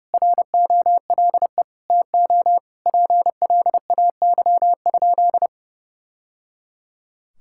Morse code, heard as a keyed tone, about 30 wpm, 700 Hz. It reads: ROLE TO PLAY?
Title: Sentences from Top 1000 Words - Review Part 6 - 30wpm